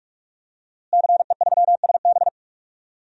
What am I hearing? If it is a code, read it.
Morse code, heard as a keyed tone, about 46 wpm, 700 Hz. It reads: CE3SB
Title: DXCC-100 Call Signs - 5 Characters - Repeated 2x - 45wpm